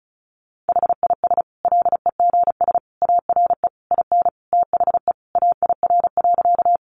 Morse code, heard as a keyed tone, about 34 wpm, 700 Hz. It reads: HIS LEGS ARE IN THE AIR.
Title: Additional Easy Phrases 35wpm